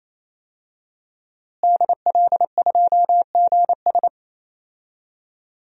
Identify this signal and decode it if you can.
Morse code, heard as a keyed tone, about 28 wpm, 700 Hz. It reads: DL2GS
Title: DXCC-100 Call Signs - 5 Characters - Repeated 3x - 28wpm